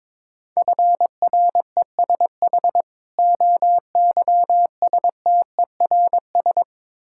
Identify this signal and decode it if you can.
Morse code, heard as a keyed tone, about 22 wpm, 690 Hz. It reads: FRESH OYSTERS
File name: Difficult Words with Hints - 22wpm